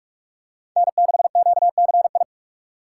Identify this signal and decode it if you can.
Morse code, heard as a keyed tone, about 42 wpm, 700 Hz. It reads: N6QCI
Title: USA Call Signs - 5 Characters - 2x Character Spacing - 45wpm